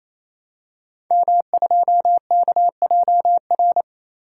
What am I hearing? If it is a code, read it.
Morse code, heard as a keyed tone, about 29 wpm, 700 Hz. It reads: M2KJR
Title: Code Groups - 5 Letter:Numbers - 28wpm